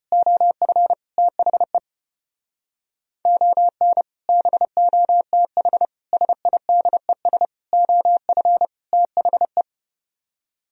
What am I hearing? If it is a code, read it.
Morse code, heard as a keyed tone, about 33 wpm, 700 Hz. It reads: OF THE ON BOTH SIDES OF THE